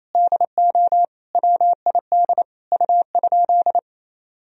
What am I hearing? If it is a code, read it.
Morse code, heard as a keyed tone, about 28 wpm, 700 Hz. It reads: DO WID U?
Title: Sentences from Top 100 Words and Abbreviations 28wpm